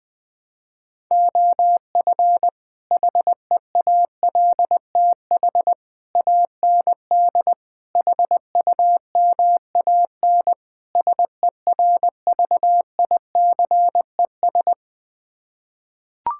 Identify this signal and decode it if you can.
Morse code, heard as a keyed tone, about 20 wpm, 700 Hz. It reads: OF HEALTH AND HUMAN SERVICES
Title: Build A Phrase - 5-Word Phrases - 20wpm